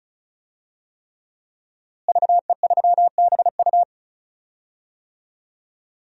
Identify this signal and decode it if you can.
Morse code, heard as a keyed tone, about 36 wpm, 700 Hz. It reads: VE3BU
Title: DXCC-100 Call Signs - 5 Characters - Repeated 3x - 35wpm